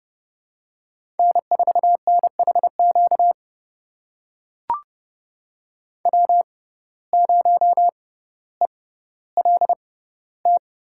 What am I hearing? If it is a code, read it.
Morse code, heard as a keyed tone, about 30 wpm, spaced as for 15 wpm, 700 Hz. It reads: N4NHQ W0ELT
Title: USA Call Signs - 5 Characters - 6x Character Spacing - 30wpm